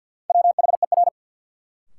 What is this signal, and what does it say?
Morse code, heard as a keyed tone, about 52 wpm, 700 Hz. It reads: WHER
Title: Sentences from Top 1500 Words - New Word Drill - 50wpm